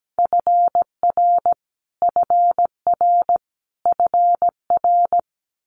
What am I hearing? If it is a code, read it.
Morse code, heard as a keyed tone, about 17 wpm, 700 Hz. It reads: FR FR FR
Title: Multi-letter Phonemes Repeated 3x 17wpm